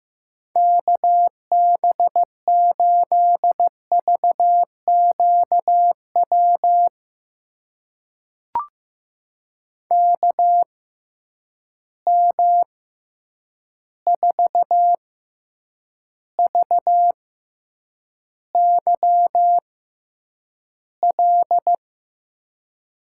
Morse code, heard as a keyed tone, about 15 wpm, spaced as for 8 wpm, 700 Hz. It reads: KB8VQW KM4VYL
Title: USA Call Signs - 6 Characters - 6x Character Spacing - 15wpm